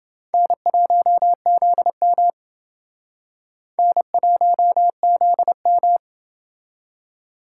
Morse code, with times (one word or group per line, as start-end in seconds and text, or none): 0.3-2.3 N1ZM
3.8-6.0 N1ZM